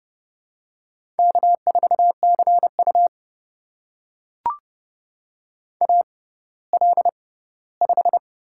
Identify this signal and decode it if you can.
Morse code, heard as a keyed tone, about 32 wpm, spaced as for 17 wpm, 700 Hz. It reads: K4CU AL5
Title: USA Call Signs - 4 Characters - 6x Character Spacing - 30wpm